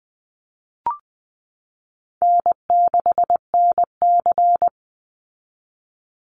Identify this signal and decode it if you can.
Morse code, heard as a keyed tone, about 20 wpm, 700 Hz. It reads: N6NC
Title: Call Signs Repeated 3x - 20wpm